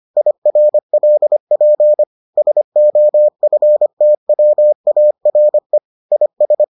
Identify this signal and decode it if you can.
Morse code, heard as a keyed tone, about 25 wpm, 600 Hz. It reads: IRLP SOFTWARE IS